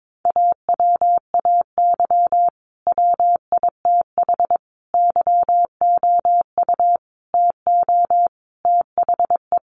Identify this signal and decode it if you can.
Morse code, heard as a keyed tone, about 22 wpm, 700 Hz. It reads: AWAY WITH YOU TO THE
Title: Sentences from Top 1400 Words 22wpm